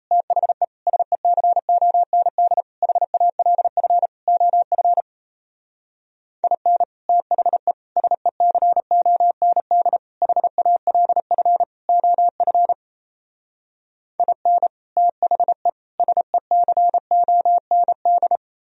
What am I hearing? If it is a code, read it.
Morse code, heard as a keyed tone, about 38 wpm, 700 Hz. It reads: THE SECOND HALF OF IN THE SECOND HALF OF IN THE SECOND